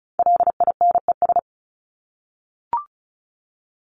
Morse code, heard as a keyed tone, about 34 wpm, 700 Hz. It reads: LINES
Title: ES:S Suffix - Top 100 Words with Base Words 35wpm